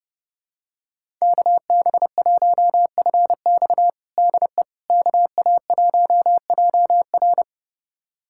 Morse code, heard as a keyed tone, about 30 wpm, 700 Hz. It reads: KB1FX DE KA1JR
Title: USA Callsign DE Callsign - 5 Characters - 30wpm